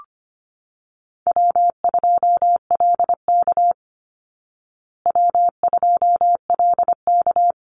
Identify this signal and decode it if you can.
Morse code, heard as a keyed tone, about 25 wpm, 700 Hz. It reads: W2LK W2LK
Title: Call Signs Repeated 3x - 25wpm